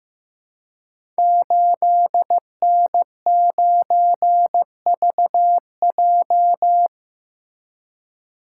Morse code, heard as a keyed tone, about 15 wpm, 700 Hz. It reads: 8N9VJ
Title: Code Groups - 5 Letter:Numbers - 15wpm